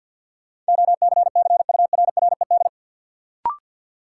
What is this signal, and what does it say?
Morse code, heard as a keyed tone, about 53 wpm, 700 Hz. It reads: OCCURRED
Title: Sets of 5 Words Encore - New Word Drill - 50wpm